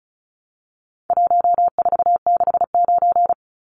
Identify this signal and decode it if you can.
Morse code, heard as a keyed tone, about 34 wpm, 700 Hz. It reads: 1469
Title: Numbers - 4-Digit 35wpm